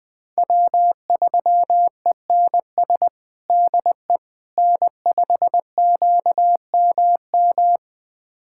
Morse code, heard as a keyed tone, about 21 wpm, 700 Hz. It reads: W3ENS DE N5QMM
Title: USA Callsign DE Callsign - 5 Characters - 20wpm